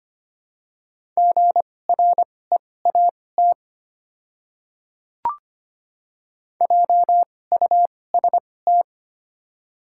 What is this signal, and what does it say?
Morse code, heard as a keyed tone, about 25 wpm, spaced as for 18 wpm, 700 Hz. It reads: GREAT JUST
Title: 2x Character Spacing - Top 300 Words - 25wpm